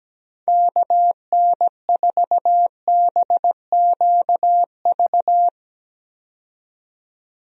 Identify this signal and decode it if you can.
Morse code, heard as a keyed tone, about 17 wpm, 700 Hz. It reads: KN4BQV
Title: USA Call Signs - 6 Characters - 17wpm